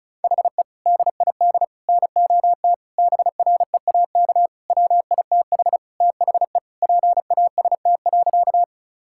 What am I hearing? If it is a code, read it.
Morse code, heard as a keyed tone, about 36 wpm, 700 Hz. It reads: HE DID NOT BREAK WITH THE PAST.
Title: Sentences from Top 1000 Words - Review Part 6 - 35wpm